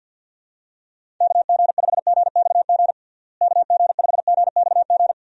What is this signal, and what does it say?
Morse code, heard as a keyed tone, about 49 wpm, 700 Hz. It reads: KG5GXG KG5GXG
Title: USA Call Signs - 6 Characters - Repeated 2x - 50wpm